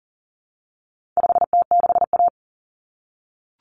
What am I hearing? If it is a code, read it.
Morse code, heard as a keyed tone, about 40 wpm, 700 Hz. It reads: 5T6A
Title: Common Cut Numbers - 4-Digit 40wpm